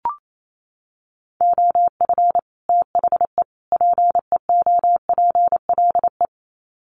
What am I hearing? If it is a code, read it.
Morse code, heard as a keyed tone, about 28 wpm, 700 Hz. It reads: OF THE PEOPLE
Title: Sets of 3 Words 28wpm